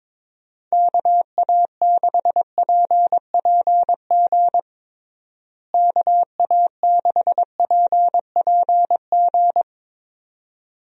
Morse code, heard as a keyed tone, about 22 wpm, 700 Hz. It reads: KA6PPG KA6PPG
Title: USA Call Signs - 6 Characters - Repeated 2x - 22wpm